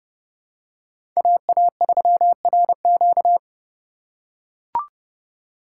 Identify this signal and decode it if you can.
Morse code, heard as a keyed tone, about 30 wpm, 700 Hz. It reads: AA3RQ
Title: USA Call Signs - 2x Character Spacing - 30wpm